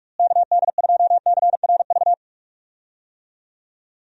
Morse code, heard as a keyed tone, about 42 wpm, 700 Hz. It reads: KD2CRV
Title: USA Call Signs - 6 Characters - Repeated 2x - 45wpm